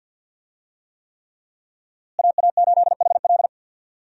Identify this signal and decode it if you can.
Morse code, heard as a keyed tone, about 46 wpm, 700 Hz. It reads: AA8HL
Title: USA Call Signs - 5 Characters - Repeated 3x - 50wpm